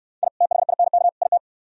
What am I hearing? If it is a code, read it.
Morse code, heard as a keyed tone, about 64 wpm, 700 Hz. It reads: I THINK IT